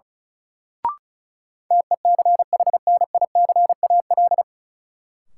Morse code, heard as a keyed tone, about 36 wpm, 700 Hz. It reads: TECHNICAL
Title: Sentences from Top 1100 Words - New Word Drill Alt - 35wpm